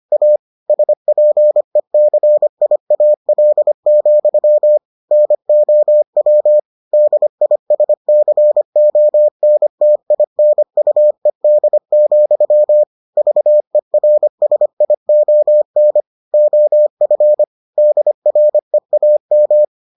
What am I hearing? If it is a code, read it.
Morse code, heard as a keyed tone, about 25 wpm, 600 Hz. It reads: A SPECIAL, NOW DISCONTINUED, VERSION OF DREAM